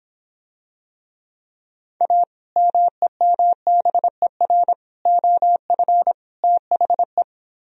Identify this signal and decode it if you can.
Morse code, heard as a keyed tone, about 26 wpm, 700 Hz. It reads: A MEMBER OF THE